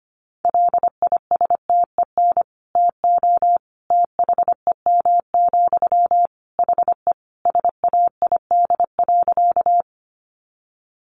Morse code, heard as a keyed tone, about 25 wpm, 700 Hz. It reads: LISTEN TO THEM, HE SAID.
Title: Sentences from Top 400 Words 25wpm